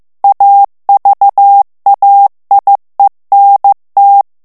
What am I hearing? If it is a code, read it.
Morse code, heard as a keyed tone, about 15 wpm, 800 Hz. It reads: AVAIENT
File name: morse mp3 / FRANCAIS TEXTE CLAIR 2 VITESSE 14 MOTS MINUTES